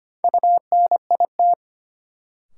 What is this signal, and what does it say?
Morse code, heard as a keyed tone, about 25 wpm, 700 Hz. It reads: UNIT
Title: Sentences from Top 1000 Words - Review Part 2 - New Word Drill - 25wpm